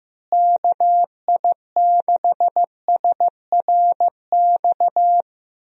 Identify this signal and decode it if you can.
Morse code, heard as a keyed tone, about 16 wpm, 700 Hz. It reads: KI6SRX
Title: USA Call Signs - 6 Characters - Repeated 2x - 15wpm